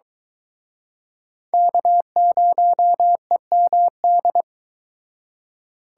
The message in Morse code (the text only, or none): K0EMD